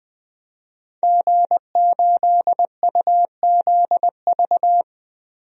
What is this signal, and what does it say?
Morse code, heard as a keyed tone, about 20 wpm, 700 Hz. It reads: G8UZV
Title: DXCC-100 Call Signs - 5 Characters - 20wpm